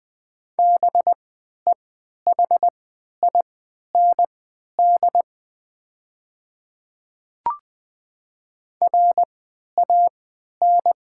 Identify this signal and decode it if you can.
Morse code, heard as a keyed tone, about 20 wpm, spaced as for 11 wpm, 700 Hz. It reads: BEHIND RAN
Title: 3x Character Spacing - Top 500 Words - 20wpm